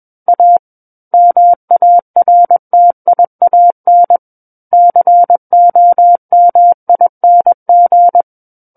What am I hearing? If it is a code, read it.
Morse code, heard as a keyed tone, about 21 wpm, 700 Hz. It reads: A MARTIAN COMING
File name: Book One / 12-Chapter 12 - 21 WPM